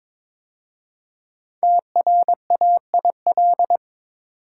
Morse code, heard as a keyed tone, about 22 wpm, 700 Hz. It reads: TRAIL
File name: 5 Letter Words - First 3 Letters Given - 22wpm